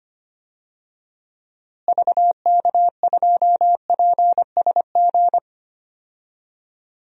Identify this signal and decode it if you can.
Morse code, heard as a keyed tone, about 25 wpm, 700 Hz. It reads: VK2PSG
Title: DXCC-100 Call Signs - 6 Characters - Repeated 2x - 25wpm